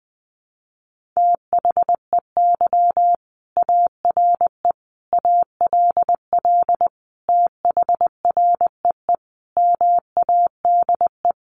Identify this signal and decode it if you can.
Morse code, heard as a keyed tone, about 20 wpm, 700 Hz. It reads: THEY ARE ALL THREE MADE
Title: Sentences from Top 1500 Words 20wpm